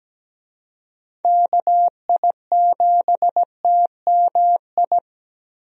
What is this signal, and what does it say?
Morse code, heard as a keyed tone, about 17 wpm, 700 Hz. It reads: KI7TMI
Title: USA Call Signs - 6 Characters - 17wpm